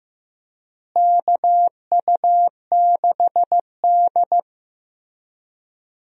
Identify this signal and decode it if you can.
Morse code, heard as a keyed tone, about 15 wpm, 700 Hz. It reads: KU6D
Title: USA Call Signs - 4 Characters - Repeated 3x - 15wpm